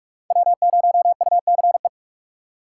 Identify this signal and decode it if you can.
Morse code, heard as a keyed tone, about 44 wpm, 700 Hz. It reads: W0UCE
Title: K1USN SST QSO Practice - 45wpm